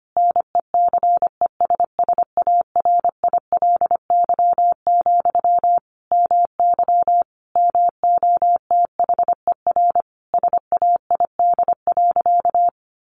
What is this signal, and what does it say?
Morse code, heard as a keyed tone, about 25 wpm, 700 Hz. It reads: NECESSARILY, MY MOTHER SAID.
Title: Sentences from Top 1400 Words 25wpm